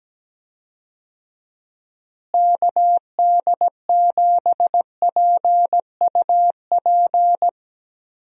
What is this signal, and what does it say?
Morse code, heard as a keyed tone, about 17 wpm, 700 Hz. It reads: KD7PUP
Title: USA Call Signs - 6 Characters - Repeated 2x - 17wpm